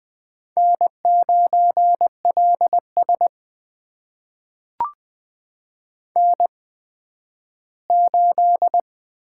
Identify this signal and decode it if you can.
Morse code, heard as a keyed tone, about 20 wpm, spaced as for 12 wpm, 700 Hz. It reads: N9LS N8
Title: USA Call Signs - 8x Character Spacing - 20wpm